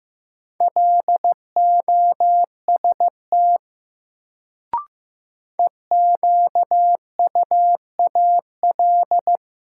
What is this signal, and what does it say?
Morse code, heard as a keyed tone, about 16 wpm, 700 Hz. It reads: LOST EQUAL
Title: Sentences from Top 1400 Words - New Word Drill - 15wpm